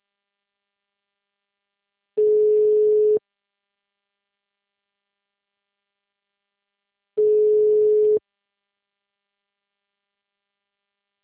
Ringback tone (Europe, 425 Hz).